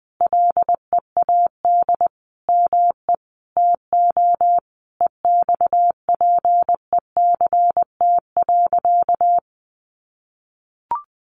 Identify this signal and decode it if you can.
Morse code, heard as a keyed tone, about 20 wpm, 700 Hz. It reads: LEAD ME TO EXPECT.
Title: Sentences from Top 1200 Words 20wpm